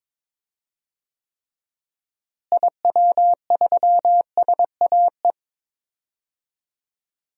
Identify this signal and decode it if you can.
Morse code, heard as a keyed tone, about 22 wpm, 700 Hz. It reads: IW3SAE
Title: DXCC-100 Call Signs - 6 Characters - Repeated 3x - 22wpm